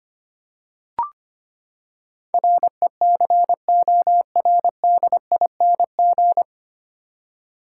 Morse code, heard as a keyed tone, about 25 wpm, 700 Hz. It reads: RECORDING
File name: Nine Letter Words - 25wpm